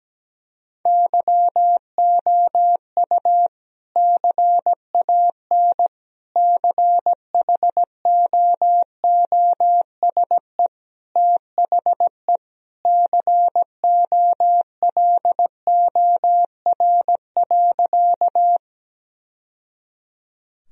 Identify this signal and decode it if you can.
Morse code, heard as a keyed tone, about 17 wpm, 700 Hz. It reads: YOU CAN CHOOSE THE COLOR.
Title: Sentences from Top 1000 Words - Review Part 4 - 17wpm